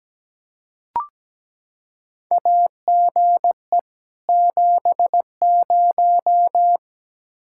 Morse code, heard as a keyed tone, about 17 wpm, 700 Hz. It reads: AGE 70
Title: Practice QSO - Exchange 3 - More Info - 17wpm